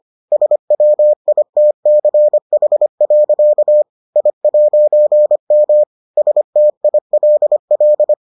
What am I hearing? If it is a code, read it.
Morse code, heard as a keyed tone, about 25 wpm, 600 Hz. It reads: SWITCH. I'M STILL